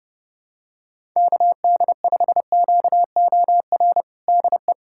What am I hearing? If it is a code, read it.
Morse code, heard as a keyed tone, about 30 wpm, 700 Hz. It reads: KD5QOR DE
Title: USA Callsign DE Callsign - 6 Characters - 30wpm